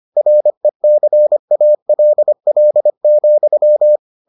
Morse code, heard as a keyed tone, about 25 wpm, 600 Hz. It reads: RECALL,